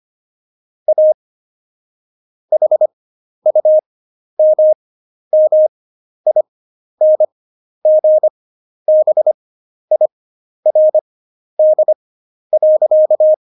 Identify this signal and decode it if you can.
Morse code, heard as a keyed tone, about 25 wpm, spaced as for 11 wpm, 620 Hz. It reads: A HUMMINGBIRD.
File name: Jokes-25-char-10-eff